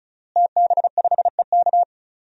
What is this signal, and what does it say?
Morse code, heard as a keyed tone, about 35 wpm, 700 Hz. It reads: TB5EK
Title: DXCC-100 Call Signs - 5 Characters - Repeated 3x - 35wpm